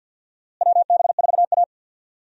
Morse code, heard as a keyed tone, about 49 wpm, 700 Hz. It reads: WB4A
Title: USA Call Signs - 4 Characters - Repeated 3x - 50wpm